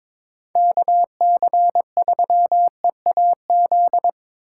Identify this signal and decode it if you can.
Morse code, heard as a keyed tone, about 22 wpm, 700 Hz. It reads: KC3EAZ
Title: USA Call Signs - 6 Characters - 4x Character Spacing - 22wpm